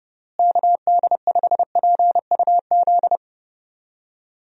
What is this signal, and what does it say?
Morse code, heard as a keyed tone, about 31 wpm, 700 Hz. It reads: KD5PUZ
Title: USA Call Signs - 6 Characters - Repeated 2x - 30wpm